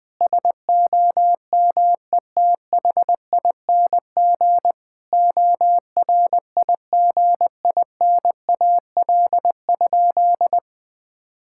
Morse code, heard as a keyed tone, about 20 wpm, 700 Hz. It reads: SOMETHING ORIGINAL?
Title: Sentences from Top 1000 Words - Review Part 6 - 20wpm